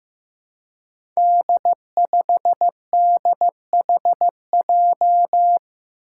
Morse code, heard as a keyed tone, about 15 wpm, 700 Hz. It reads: D5DHJ
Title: Five Challenging Letter-Number Combinations 15wpm